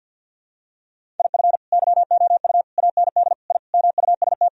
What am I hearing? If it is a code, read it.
Morse code, heard as a keyed tone, about 49 wpm, 700 Hz. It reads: IF YOU AND I MUST
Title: Sentences from Top 1000 Words - Review Part 1 - 50wpm